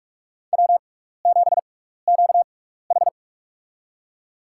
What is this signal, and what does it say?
Morse code, heard as a keyed tone, about 45 wpm, spaced as for 15 wpm, 700 Hz. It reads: W7QH
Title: USA Call Signs - 4 Characters - 6x Character Spacing - 45wpm